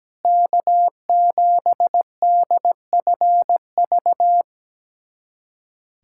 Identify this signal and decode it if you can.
Morse code, heard as a keyed tone, about 17 wpm, 700 Hz. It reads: K7DFV
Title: USA Callsign DE Callsign - 5 Characters - 17wpm